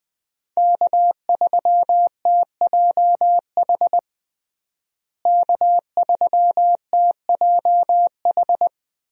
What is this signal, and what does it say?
Morse code, heard as a keyed tone, about 20 wpm, 700 Hz. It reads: K3TJH K3TJH
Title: USA Call Signs - 5 Characters - Repeated 3x - 20wpm